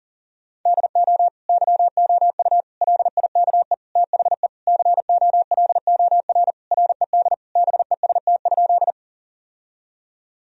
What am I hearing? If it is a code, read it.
Morse code, heard as a keyed tone, about 37 wpm, 700 Hz. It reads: DO YOU LIKE THE COLOR RED BEST?
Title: Sentences from Top 1000 Words - Review Part 2 - 40wpm